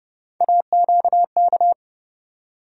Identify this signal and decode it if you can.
Morse code, heard as a keyed tone, about 30 wpm, 700 Hz. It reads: AQK
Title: USA Call Sign Postfixes - 3 Letters 30wpm